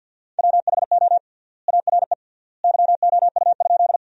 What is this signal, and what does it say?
Morse code, heard as a keyed tone, about 50 wpm, 700 Hz. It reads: WHO ARE YOU?
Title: 3x Word-Spacing - Sentences from Top 100 Words - 50wpm